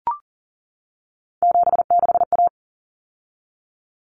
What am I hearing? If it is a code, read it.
Morse code, heard as a keyed tone, about 40 wpm, 700 Hz. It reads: 76A
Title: Common Cut Numbers - 3-Digit 40wpm